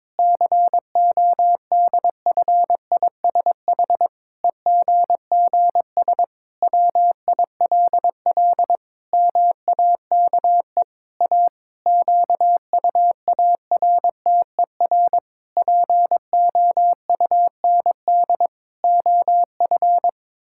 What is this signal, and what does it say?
Morse code, heard as a keyed tone, about 22 wpm, 700 Hz. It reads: CODFISH EGGS WILL MAKE A QUARTER POUND OF
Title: Pangram Challenge v2 - 22wpm